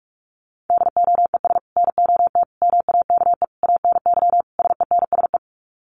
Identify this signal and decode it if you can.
Morse code, heard as a keyed tone, about 44 wpm, 700 Hz. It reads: DOES NOT MAKE ANY SENSE
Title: Sets of 5 Words Encore2 45wpm